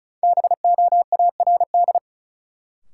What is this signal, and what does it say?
Morse code, heard as a keyed tone, about 36 wpm, 700 Hz. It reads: BOARD
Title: Sentences from Top 700 Words - New Word Drill - 35wpm